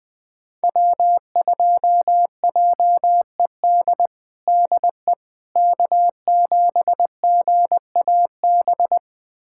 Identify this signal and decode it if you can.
Morse code, heard as a keyed tone, about 21 wpm, 700 Hz. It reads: W2JED DE K7GAB
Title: USA Callsign DE Callsign - 5 Characters - 20wpm